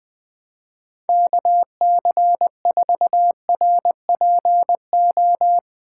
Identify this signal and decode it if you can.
Morse code, heard as a keyed tone, about 21 wpm, 700 Hz. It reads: KC4RPO